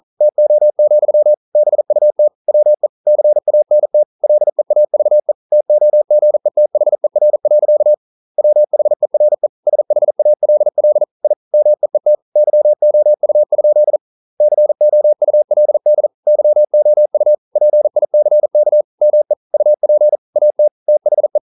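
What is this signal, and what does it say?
Morse code, heard as a keyed tone, about 42 wpm, 600 Hz. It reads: TO, BUT WE CANT LEAVE TOGETHER. WHERE SHALL I MEET YOU? COULD YOU PICK ME UP AT THE
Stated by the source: Morse code practice at 41 wpm, from CW med Gnister.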